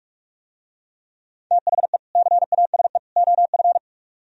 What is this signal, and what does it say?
Morse code, heard as a keyed tone, about 42 wpm, 700 Hz. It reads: THE CASE OF